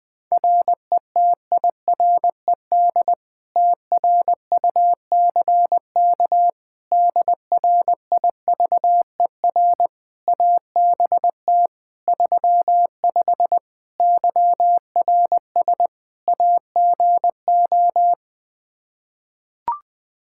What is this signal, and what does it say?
Morse code, heard as a keyed tone, about 20 wpm, 700 Hz. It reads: RETIRED TRUCK DRIVER ABT 35 YRS AGO
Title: Practice QSO - Exchange 3 - More Info - 20wpm